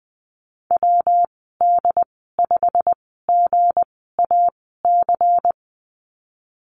Morse code, heard as a keyed tone, about 20 wpm, spaced as for 14 wpm, 700 Hz. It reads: WD5GAC
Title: 193 Lesson 31g - Letter v - Call Signs 20wpm